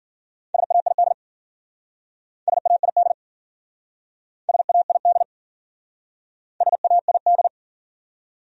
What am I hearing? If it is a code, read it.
Morse code, heard as a keyed tone, about 52 wpm, 700 Hz. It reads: SAID SAID SAID SAID